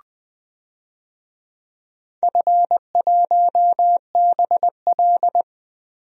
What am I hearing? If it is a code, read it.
Morse code, heard as a keyed tone, about 20 wpm, 700 Hz. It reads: F1BL
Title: DXCC-100 Call Signs - 4 Characters - 20wpm